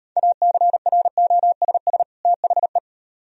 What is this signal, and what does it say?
Morse code, heard as a keyed tone, about 38 wpm, 700 Hz. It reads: ACROSS THE